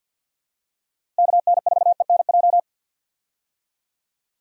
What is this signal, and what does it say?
Morse code, heard as a keyed tone, about 50 wpm, 700 Hz. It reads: KN4ENJ
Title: USA Call Signs - 6 Characters - Repeated 2x - 50wpm